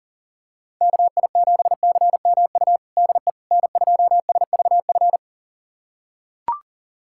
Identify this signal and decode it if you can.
Morse code, heard as a keyed tone, about 40 wpm, 700 Hz. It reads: KI7CMU DE N2SVF
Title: USA Callsign DE Callsign 40wpm